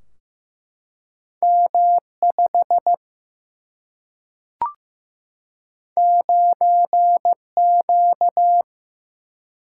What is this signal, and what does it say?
Morse code, heard as a keyed tone, about 16 wpm, 700 Hz. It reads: M5 9Q
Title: Single-Letter Single-Number 15wpm